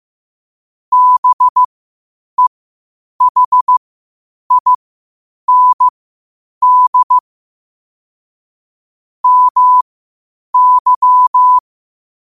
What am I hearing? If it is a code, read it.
Morse code, heard as a keyed tone, about 15 wpm, spaced as for 8 wpm, 1000 Hz. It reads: BEHIND MY